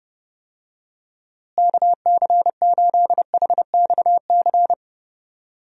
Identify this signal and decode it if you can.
Morse code, heard as a keyed tone, about 31 wpm, 700 Hz. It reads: KC8HXC
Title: USA Call Signs - 6 Characters - Repeated 2x - 30wpm